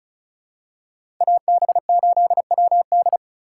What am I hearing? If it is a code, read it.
Morse code, heard as a keyed tone, about 36 wpm, 700 Hz. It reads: AB8WD
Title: USA Call Signs - 4x Character Spacing - 35wpm